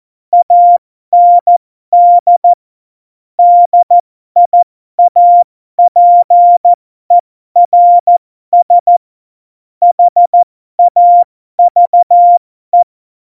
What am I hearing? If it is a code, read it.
Morse code, heard as a keyed tone, about 14 wpm, spaced as for 12 wpm, 700 Hz. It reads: AND DIAPERS HAVE
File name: Morsekurs / Witze